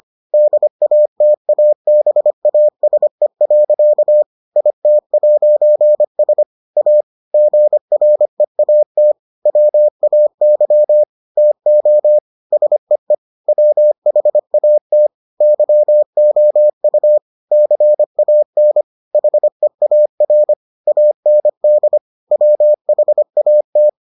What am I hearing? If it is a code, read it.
Morse code, heard as a keyed tone, about 25 wpm, 600 Hz. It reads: DATABASE. IT'S A GREAT WAY TO SEE WHAT YOU CAN HEAR AND WHAT